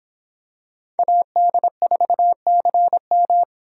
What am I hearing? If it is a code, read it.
Morse code, heard as a keyed tone, about 26 wpm, 700 Hz. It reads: AD4CM